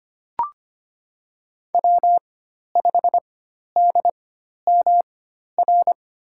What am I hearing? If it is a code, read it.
Morse code, heard as a keyed tone, about 25 wpm, spaced as for 12 wpm, 700 Hz. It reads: W5DMR
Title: USA Call Signs - 5 Characters - 4x Character Spacing - 25wpm